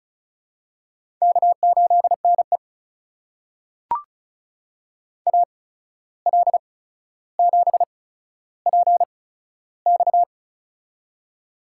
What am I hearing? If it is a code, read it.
Morse code, heard as a keyed tone, about 36 wpm, spaced as for 14 wpm, 700 Hz. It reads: K8NE AL7PX
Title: USA Call Signs - 8x Character Spacing - 35wpm